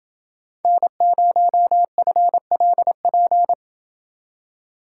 Morse code, heard as a keyed tone, about 27 wpm, 700 Hz. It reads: N0FLP